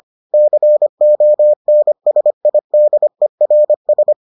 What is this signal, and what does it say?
Morse code, heard as a keyed tone, about 25 wpm, 600 Hz. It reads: CONSIDERS